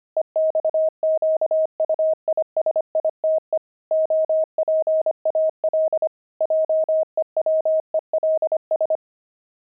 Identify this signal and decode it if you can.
Morse code, heard as a keyed tone, about 25 wpm, 630 Hz. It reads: EXQUISITE OPAL JEWELS
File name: pangram-fs-25-cs-25-opal